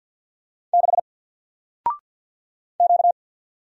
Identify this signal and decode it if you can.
Morse code, heard as a keyed tone, about 52 wpm, 700 Hz. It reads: 6 Q
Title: Single Letter-Number Alt 50wpm